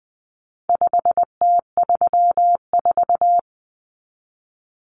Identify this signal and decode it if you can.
Morse code, heard as a keyed tone, about 20 wpm, 700 Hz. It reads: 5T34
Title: Common Cut Numbers - 4-Digit 20wpm